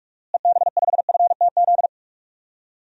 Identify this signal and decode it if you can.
Morse code, heard as a keyed tone, about 42 wpm, 700 Hz. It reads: EB5FTZ
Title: DXCC-100 Call Signs - 6 Characters - Repeated 2x - 45wpm